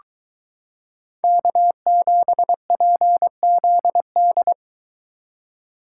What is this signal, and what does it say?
Morse code, heard as a keyed tone, about 22 wpm, 700 Hz. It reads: K7PZD